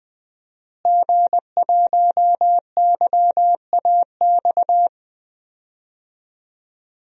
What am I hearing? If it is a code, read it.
Morse code, heard as a keyed tone, about 21 wpm, 700 Hz. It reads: G1YAX
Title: DXCC-100 Call Signs - 5 Characters - Repeated 3x - 20wpm